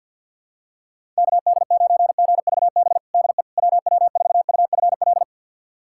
Morse code, heard as a keyed tone, about 46 wpm, 700 Hz. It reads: KD9GVB DE WW4UFL
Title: USA Callsign DE Callsign - 6 Characters - 50wpm